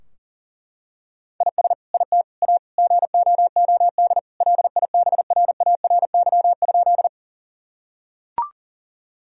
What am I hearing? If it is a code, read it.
Morse code, heard as a keyed tone, about 40 wpm, 700 Hz. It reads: IS IT A GOOD LIBRARY?
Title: Sentences from Top 1000 Words - Review Part 4 - 40wpm